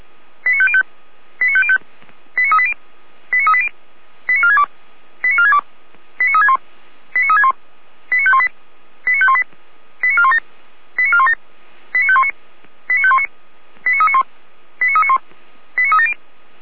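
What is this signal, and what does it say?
Five-tone selective call (ZVEI), heard as quick runs of five tones, tones of about 70 ms. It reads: 89505 89505 89380 89380 89462 89462 89361 89361 89518 89518 89417 89417 89519 89519 89391 89391 89370
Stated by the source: ZVEI five-tone call 89120 received on SDR.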